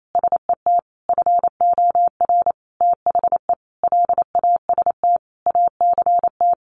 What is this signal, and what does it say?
Morse code, heard as a keyed tone, about 28 wpm, 700 Hz. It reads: SET FOR THE LAST ACT
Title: Sentences from Top 1500 Words 28wpm